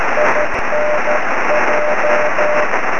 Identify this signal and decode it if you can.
Morse code, heard as a keyed tone, about 13 wpm, 590 Hz, with noise. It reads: INJ
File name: sm7lcb / 20 23cm / 1615 pa3n 23cm